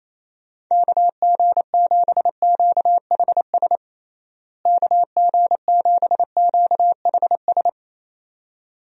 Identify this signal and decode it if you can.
Morse code, heard as a keyed tone, about 27 wpm, 700 Hz. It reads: KG7QHS KG7QHS
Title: USA Call Signs - 6 Characters - Repeated 2x - 28wpm